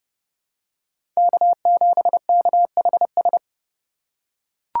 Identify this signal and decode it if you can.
Morse code, heard as a keyed tone, about 30 wpm, 700 Hz. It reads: K7KHS
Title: USA Call Signs - 5 Characters - 2x Character Spacing - 30wpm